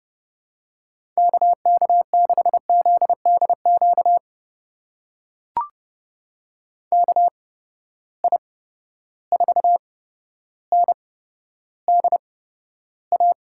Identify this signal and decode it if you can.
Morse code, heard as a keyed tone, about 30 wpm, spaced as for 13 wpm, 700 Hz. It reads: KK6ZDQ KI4NDA
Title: USA Call Signs - 6 Characters - 8x Character Spacing - 30wpm